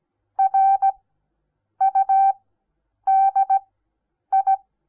Morse code, heard as a keyed tone, about 17 wpm, spaced as for 8 wpm, 780 Hz. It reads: RUDI
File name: Onlinelehrgang / morsen / DJ4UF Morsen08